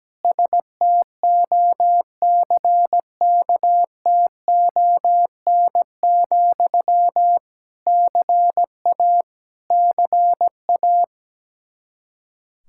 Morse code, heard as a keyed tone, about 17 wpm, 700 Hz. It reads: STOCKTON, CA CA
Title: Practice QSO - Exchange 1 - Beginning - 17wpm